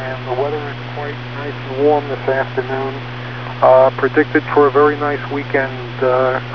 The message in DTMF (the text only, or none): none